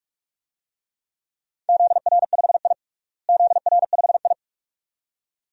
Morse code, heard as a keyed tone, about 46 wpm, 700 Hz. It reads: ZR5I ZR5I
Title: DXCC-100 Call Signs - 4 Characters - Repeated 2x - 45wpm